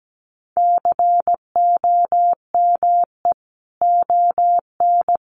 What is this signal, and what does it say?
Morse code, heard as a keyed tone, about 17 wpm, 700 Hz. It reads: COME ON
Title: Sentences from Top 1300 Words 17wpm